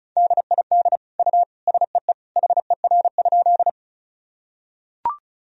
Morse code, heard as a keyed tone, about 34 wpm, 700 Hz. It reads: DID U SEE HER?
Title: Sentences from Top 100 Words and Abbreviations 35wpm